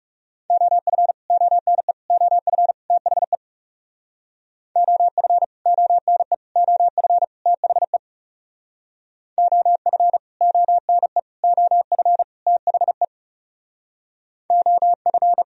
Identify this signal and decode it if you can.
Morse code, heard as a keyed tone, about 36 wpm, 700 Hz. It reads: OF ONE OF THE OF ONE OF THE OF ONE OF THE OF
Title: Speed-Racing - Sets of 4 Words 30wpm